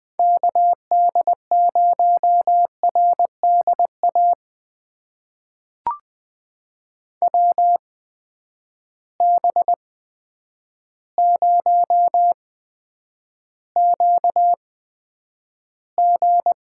Morse code, heard as a keyed tone, about 21 wpm, spaced as for 9 wpm, 700 Hz. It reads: KD0RDA WB0QG
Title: USA Call Signs - 6 Characters - 8x Character Spacing - 20wpm